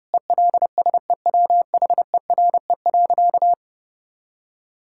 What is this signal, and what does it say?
Morse code, heard as a keyed tone, about 30 wpm, 700 Hz. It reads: ELSEWHERE.
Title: Sentences from Top 1600 Words 30wpm